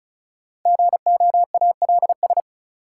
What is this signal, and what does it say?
Morse code, heard as a keyed tone, about 34 wpm, 700 Hz. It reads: GOALS
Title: Sentences from Top 1000 Words - Review Part 6 - New Word Drill - 35wpm